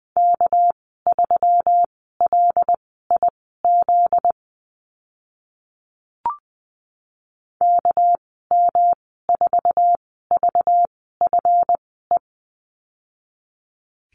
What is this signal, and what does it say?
Morse code, heard as a keyed tone, about 20 wpm, spaced as for 15 wpm, 700 Hz. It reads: K3LIZ KM4VFE
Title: 245 Lesson 39g - Letter z - Call Signs 20wpm